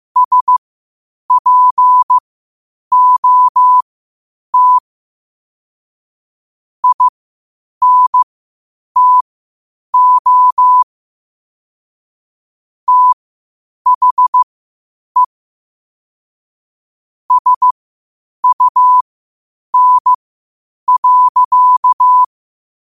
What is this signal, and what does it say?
Morse code, heard as a keyed tone, about 15 wpm, spaced as for 8 wpm, 1000 Hz. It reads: SPOT INTO THE SUN.
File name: qotdfiles / 8wpm